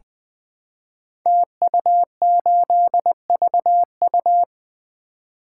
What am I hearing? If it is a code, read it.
Morse code, heard as a keyed tone, about 20 wpm, 700 Hz. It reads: TU8VU
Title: Code Groups - 5 Letter:Numbers - 20wpm